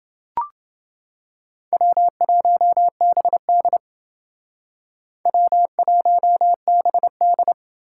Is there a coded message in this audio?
Yes: Morse code from 1.7 to 7.5 s, heard as a keyed tone, about 28 wpm, 700 Hz. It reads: W1BD W1BD